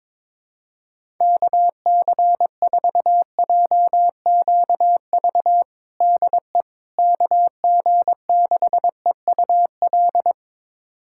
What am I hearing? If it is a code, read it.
Morse code, heard as a keyed tone, about 22 wpm, 700 Hz. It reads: KC4JQV DE KG6EUL
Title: USA Callsign DE Callsign - 6 Characters - 22wpm